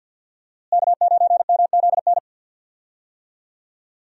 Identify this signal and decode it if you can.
Morse code, heard as a keyed tone, about 52 wpm, 700 Hz. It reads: K9MZN
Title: USA Call Signs - 5 Characters - Repeated 3x - 50wpm